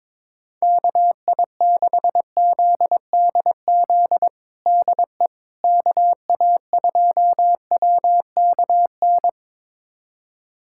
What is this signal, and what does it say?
Morse code, heard as a keyed tone, about 21 wpm, 700 Hz. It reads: KI6ZDZ DE KA2WKN
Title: USA Callsign DE Callsign 22wpm